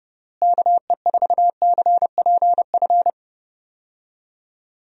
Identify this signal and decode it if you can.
Morse code, heard as a keyed tone, about 31 wpm, 700 Hz. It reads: KE4CPF
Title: USA Call Signs - 6 Characters - Repeated 2x - 30wpm